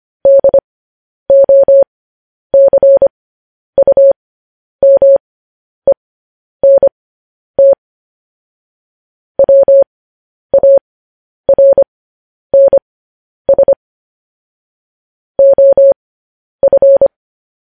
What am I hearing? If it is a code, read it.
Morse code, heard as a keyed tone, about 25 wpm, spaced as for 10 wpm, 550 Hz. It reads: DOCUMENT WARNS OF